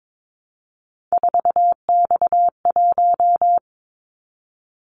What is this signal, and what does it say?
Morse code, heard as a keyed tone, about 22 wpm, 700 Hz. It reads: 4X1
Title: Callsign Prefixes for Top 50 DXCC - 22wpm